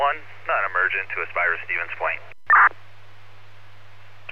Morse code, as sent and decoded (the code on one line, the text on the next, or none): none